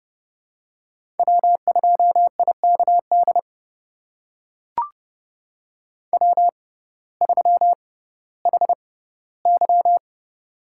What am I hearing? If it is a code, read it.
Morse code, heard as a keyed tone, about 30 wpm, spaced as for 16 wpm, 700 Hz. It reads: W2IKD W3HY
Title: USA Call Signs - 6x Character Spacing - 30wpm